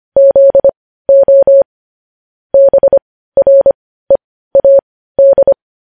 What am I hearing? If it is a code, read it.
Morse code, heard as a keyed tone, about 25 wpm, spaced as for 15 wpm, 550 Hz. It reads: ZO BREAD
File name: podcast / News headlines-1x-25x15